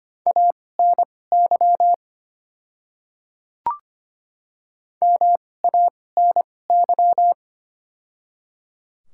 Morse code, heard as a keyed tone, about 25 wpm, spaced as for 18 wpm, 690 Hz. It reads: ANY MANY
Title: 2x Character Spacing - Top 200 Words - 25wpm